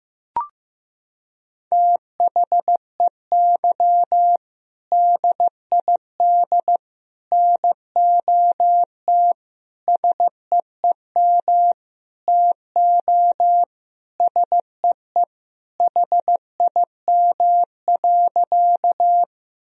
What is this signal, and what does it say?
Morse code, heard as a keyed tone, about 15 wpm, 700 Hz. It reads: THEY DID NOT SEEM TO SEE HIM.
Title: Sentences from Top 1000 Words - Review Part 1 - 15wpm